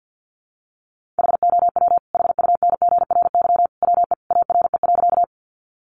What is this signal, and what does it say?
Morse code, heard as a keyed tone, about 50 wpm, 700 Hz. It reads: HOW HUNGRY WE ARE.
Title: Sentences from Top 1400 Words 50wpm